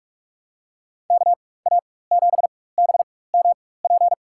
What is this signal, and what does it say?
Morse code, heard as a keyed tone, about 44 wpm, spaced as for 21 wpm, 700 Hz. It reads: KA7BMP
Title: USA Call Signs - 6 Characters - 4x Character Spacing - 45wpm